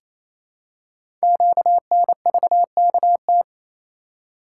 Morse code, heard as a keyed tone, about 28 wpm, 700 Hz. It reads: QNVKT
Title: Code Groups - 5 Letter:Numbers - 28wpm